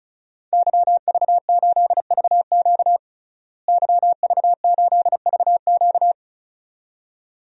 Morse code, heard as a keyed tone, about 36 wpm, 700 Hz. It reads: YV8VQ YV8VQ
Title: DXCC-100 Call Signs - 5 Characters - Repeated 2x - 35wpm